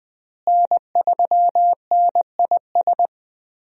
Morse code, heard as a keyed tone, about 20 wpm, 700 Hz. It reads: N3NIS
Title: USA Call Signs - 5 Characters - Repeated 3x - 20wpm